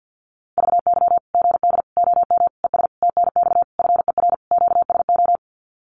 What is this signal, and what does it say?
Morse code, heard as a keyed tone, about 50 wpm, 700 Hz. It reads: VY GD OM ES TNX FER QSO
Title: Common QSO Phrases 50wpm